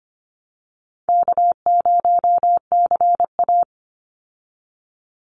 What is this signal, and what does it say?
Morse code, heard as a keyed tone, about 25 wpm, 700 Hz. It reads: K0CA
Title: Call Signs Repeated 2x - 25wpm